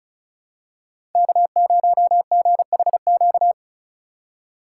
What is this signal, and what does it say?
Morse code, heard as a keyed tone, about 36 wpm, 700 Hz. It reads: K0GHQ
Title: USA Call Signs - 5 Characters - 4x Character Spacing - 35wpm